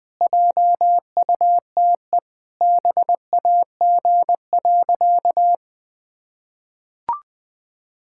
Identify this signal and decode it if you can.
Morse code, heard as a keyed tone, about 21 wpm, 700 Hz. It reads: JUTE BAG.
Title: Pangram Challenge v2 - 20wpm